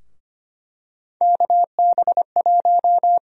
Morse code, heard as a keyed tone, about 25 wpm, 700 Hz. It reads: KB1
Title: USA Call Sign Prefixes - 25wpm